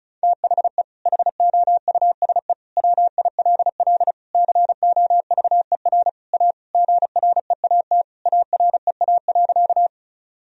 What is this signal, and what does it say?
Morse code, heard as a keyed tone, about 34 wpm, 700 Hz. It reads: THE HOUSE WILL COVER A GREAT AREA.
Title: Sentences from Top 1000 Words - Review Part 2 - 35wpm